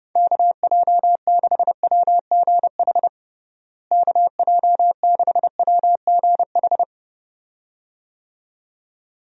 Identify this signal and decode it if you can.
Morse code, heard as a keyed tone, about 30 wpm, 700 Hz. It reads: KJ6WGH KJ6WGH
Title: USA Call Signs - 6 Characters - Repeated 2x - 30wpm